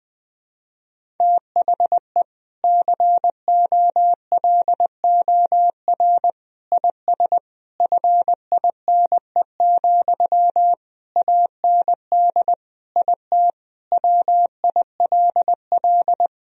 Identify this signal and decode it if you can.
Morse code, heard as a keyed tone, about 21 wpm, 700 Hz. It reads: THE COLOR IS FINE, AND IT WILL
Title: Sentences from Top 1000 Words - Review Part 2 - 20wpm